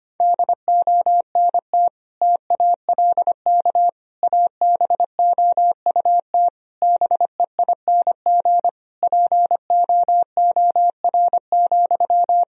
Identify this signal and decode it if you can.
Morse code, heard as a keyed tone, about 25 wpm, 700 Hz. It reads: DONT TALK ABOUT BEING POOR,
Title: Sentences from Top 1000 Words - Review Part 4 - 25wpm